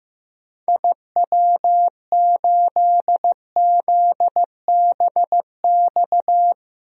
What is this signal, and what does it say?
Morse code, heard as a keyed tone, about 15 wpm, 700 Hz. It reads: IW8ZBX
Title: DXCC-100 Call Signs - 6 Characters - 15wpm